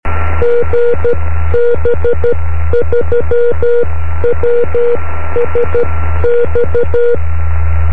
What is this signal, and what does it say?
Morse code, heard as a keyed tone, about 12 wpm, spaced as for 11 wpm, 480 Hz. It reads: GB3WSX
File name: om3pv 70mhz / zaznam / gb3wsx